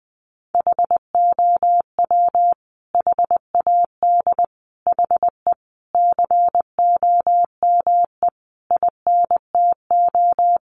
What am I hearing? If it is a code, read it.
Morse code, heard as a keyed tone, about 20 wpm, 700 Hz. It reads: HOW HAD HE COME INTO
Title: Sentences from Top 1300 Words 20wpm